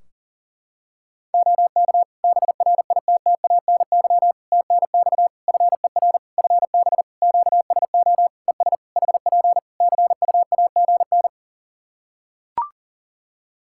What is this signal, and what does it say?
Morse code, heard as a keyed tone, about 40 wpm, 700 Hz. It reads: OK BRITTANY TNX FER FB QSO ES HP CUAGN
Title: K1USN SST QSO Practice 40wpm